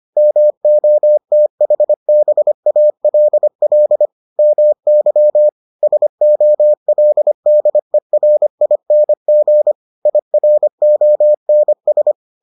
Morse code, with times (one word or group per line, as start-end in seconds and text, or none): 0.2-4.1 MOTHBALL
4.4-5.5 MY
5.8-9.7 SOLDERING
10.1-12.1 IRONS